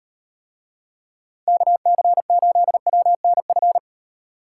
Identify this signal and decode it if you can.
Morse code, heard as a keyed tone, about 38 wpm, 700 Hz. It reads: KC8WNF